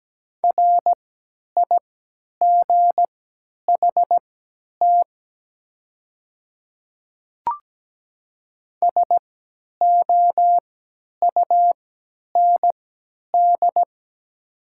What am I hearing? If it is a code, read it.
Morse code, heard as a keyed tone, about 17 wpm, spaced as for 10 wpm, 700 Hz. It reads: RIGHT SOUND
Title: 3x Character Spacing - Top 500 Words - 17wpm